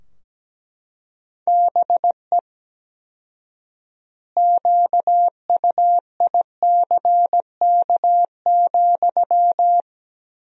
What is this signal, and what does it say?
Morse code, heard as a keyed tone, about 17 wpm, 700 Hz. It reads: BE QUICK,
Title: 4x Word-Spacing - Sentences from Top 500 Words - 17wpm